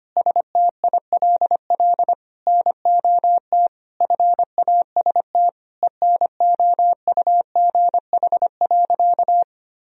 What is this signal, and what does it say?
Morse code, heard as a keyed tone, about 25 wpm, 700 Hz. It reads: STILL NOT FAST ENOUGH.
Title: Sentences from Top 1000 Words - Review Part 3 - 25wpm